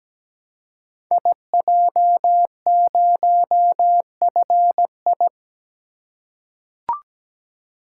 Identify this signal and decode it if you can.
Morse code, heard as a keyed tone, about 17 wpm, 700 Hz. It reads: IJ0FI
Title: Code Groups - 5 Letter:Numbers - 17wpm